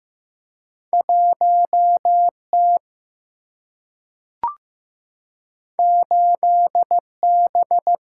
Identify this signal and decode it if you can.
Morse code, heard as a keyed tone, about 15 wpm, 700 Hz. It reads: 1 T 8 B
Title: Single-Letter Single-Number 15wpm